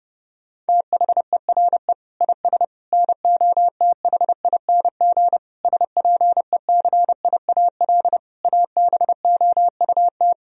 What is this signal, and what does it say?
Morse code, heard as a keyed tone, about 30 wpm, 700 Hz. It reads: THERE IS NOTHING SPECIAL ABOUT
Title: Sentences from Top 1000 Words - Review Part 3 - 30wpm